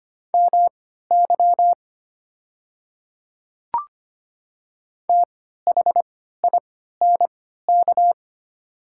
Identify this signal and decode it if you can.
Morse code, heard as a keyed tone, about 25 wpm, spaced as for 14 wpm, 690 Hz. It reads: MY THINK
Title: 3x Character Spacing - Top 300 Words - 25wpm